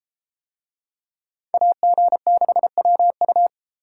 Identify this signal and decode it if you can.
Morse code, heard as a keyed tone, about 31 wpm, 700 Hz. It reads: AG6WU